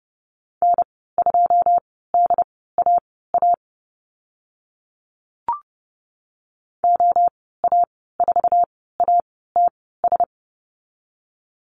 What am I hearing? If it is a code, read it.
Morse code, heard as a keyed tone, about 30 wpm, spaced as for 17 wpm, 700 Hz. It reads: N2DAA OA4ATS
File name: Lesson 14g - Number 2 - Call Signs 30wpm